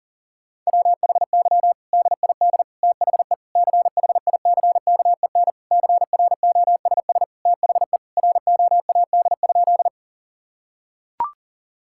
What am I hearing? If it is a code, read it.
Morse code, heard as a keyed tone, about 40 wpm, 700 Hz. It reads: WHY DID THE CHICKEN CROSS THE ROAD?
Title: Sentences from Top 1000 Words - Review Part 5 - 40wpm